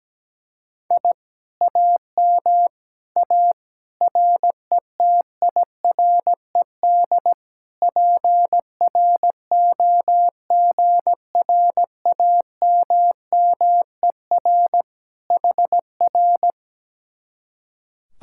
Morse code, heard as a keyed tone, about 17 wpm, 700 Hz. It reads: I AM A RETIRED PROGRAMMER HR
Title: Speed-Racing Common QSO Phrases Encore - 17wpm